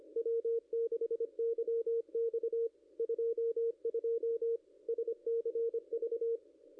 Morse code, heard as a keyed tone, about 25 wpm, 440 Hz, with radio noise. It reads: W6YX 22 SCV
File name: contests / ss-cw-k5tr-op-n5rz -0500